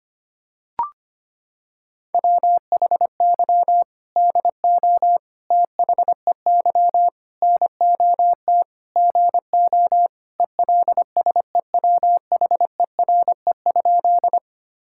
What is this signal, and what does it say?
Morse code, heard as a keyed tone, about 25 wpm, 700 Hz. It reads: WHY DO THEY NOT GO ELSEWHERE?
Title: Sentences from Top 1600 Words 25wpm